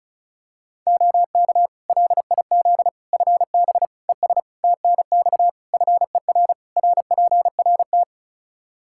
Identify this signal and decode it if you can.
Morse code, heard as a keyed tone, about 34 wpm, 700 Hz. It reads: OK LIZ FB ES TNX FER RPRT
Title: K1USN SST QSO Practice 35wpm